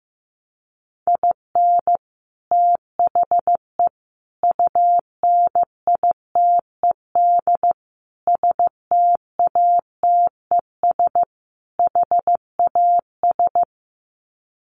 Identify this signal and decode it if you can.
Morse code, heard as a keyed tone, about 15 wpm, 700 Hz. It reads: IN THE UNITED STATES HAS
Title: Sets of 5 Words Encore2 15wpm